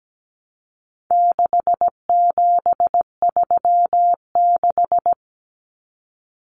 Morse code, heard as a keyed tone, about 17 wpm, 700 Hz. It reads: 6736
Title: Common Cut Numbers - 4-Digit 17wpm